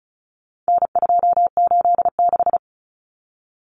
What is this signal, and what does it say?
Morse code, heard as a keyed tone, about 34 wpm, 700 Hz. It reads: N286
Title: Common Cut Numbers - 4-Digit 35wpm